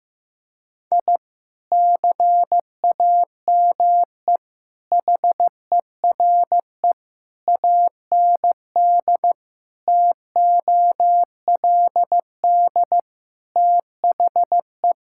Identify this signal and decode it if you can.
Morse code, heard as a keyed tone, about 16 wpm, 700 Hz. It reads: I CAME HERE AND TOLD THE
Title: Sentences from Top 1000 Words - Review Part 6 - 15wpm